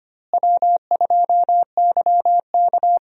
Morse code, heard as a keyed tone, about 25 wpm, 700 Hz. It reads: W2YK
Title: USA Call Signs - 4 Characters - Repeated 2x - 25wpm